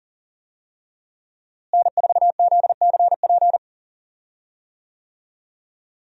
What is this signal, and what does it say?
Morse code, heard as a keyed tone, about 40 wpm, 700 Hz. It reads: N4ZCP
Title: USA Call Signs - 5 Characters - Repeated 2x - 40wpm